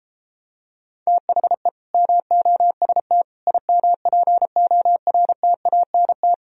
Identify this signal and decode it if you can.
Morse code, heard as a keyed tone, about 32 wpm, 700 Hz. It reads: THE MOST IMPORTANT